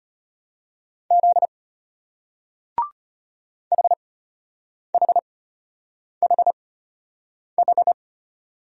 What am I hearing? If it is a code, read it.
Morse code, heard as a keyed tone, about 34 wpm, spaced as for 8 wpm, 700 Hz. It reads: Z HHHH